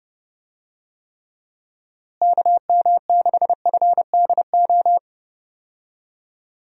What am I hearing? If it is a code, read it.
Morse code, heard as a keyed tone, about 30 wpm, 700 Hz. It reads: KM6FDO
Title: USA Call Signs - 6 Characters - Repeated 2x - 30wpm